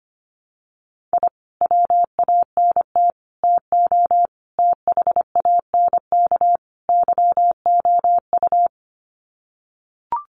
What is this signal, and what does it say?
Morse code, heard as a keyed tone, about 25 wpm, 700 Hz. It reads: I WANT TO THANK YOU
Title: Sets of 5 Words Encore 25wpm